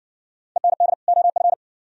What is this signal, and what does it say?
Morse code, heard as a keyed tone, about 60 wpm, 690 Hz. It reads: END OF